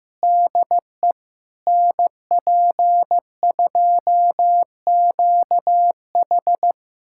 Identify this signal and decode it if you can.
Morse code, heard as a keyed tone, about 15 wpm, 700 Hz. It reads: DE NP2QH
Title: USA Callsign DE Callsign - 5 Characters - 15wpm